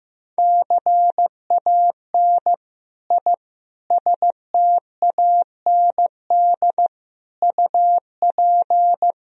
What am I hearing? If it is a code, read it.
Morse code, heard as a keyed tone, about 15 wpm, 700 Hz. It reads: CAN I STAND UP
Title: Sentences from Top 1000 Words - Review Part 1 - 15wpm